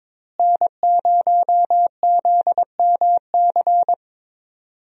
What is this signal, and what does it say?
Morse code, heard as a keyed tone, about 22 wpm, 700 Hz. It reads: N0ZMC
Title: USA Call Signs - 5 Characters - Repeated 3x - 22wpm